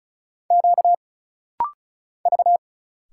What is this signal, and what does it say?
Morse code, heard as a keyed tone, about 34 wpm, 700 Hz. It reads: Q V
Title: Single Letters - Rapid-Fire 35wpm